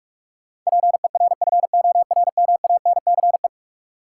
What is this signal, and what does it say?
Morse code, heard as a keyed tone, about 47 wpm, 700 Hz. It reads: PERFORMANCE
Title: Sentences from Top 1000 Words - New Word Drill - 45wpm